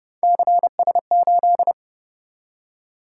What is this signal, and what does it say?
Morse code, heard as a keyed tone, about 29 wpm, 700 Hz. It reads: CS8